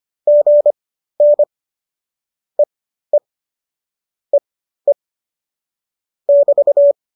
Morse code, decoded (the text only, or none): GN EE EE <BT>